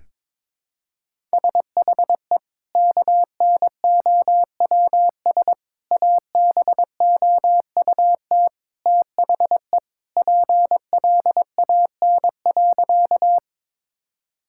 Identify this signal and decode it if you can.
Morse code, heard as a keyed tone, about 22 wpm, 700 Hz. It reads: SHE KNOWS ABOUT THE PLAN.
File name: practice / 22wpm / Sentences with 2-Word Phrases - 22wpm